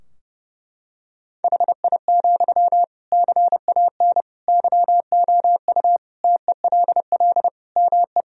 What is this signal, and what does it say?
Morse code, heard as a keyed tone, about 32 wpm, 700 Hz. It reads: HI, CAN YOU TELL ME
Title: Sentences from Top 1000 Words - Review Part 3 - 30wpm